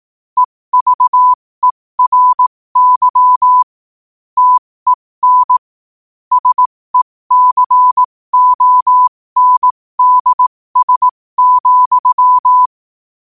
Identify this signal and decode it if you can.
Morse code, heard as a keyed tone, about 18 wpm, spaced as for 15 wpm, 1000 Hz. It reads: EVERY TEN SECONDS,
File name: qotdfiles / 15wpm